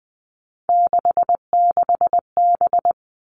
Morse code, heard as a keyed tone, about 20 wpm, 700 Hz. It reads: 66B
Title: All Cut Numbers - 3-Digit 20wpm